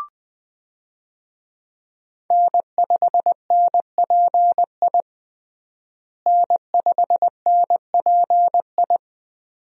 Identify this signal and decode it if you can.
Morse code, heard as a keyed tone, about 20 wpm, 700 Hz. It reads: N5NPI N5NPI
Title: USA Call Signs - 5 Characters - Repeated 2x - 20wpm